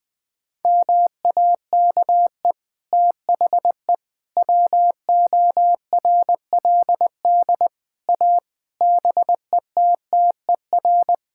Morse code, heard as a keyed tone, about 21 wpm, 700 Hz. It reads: MAKE THE WORLD A BETTER
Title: Sets of 6 Words - 20wpm